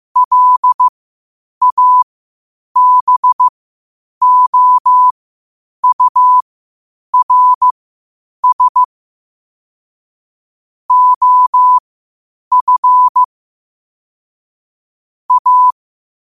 Morse code, heard as a keyed tone, about 15 wpm, spaced as for 8 wpm, 1000 Hz. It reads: LABOURS OF A
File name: qotdfiles / 8wpm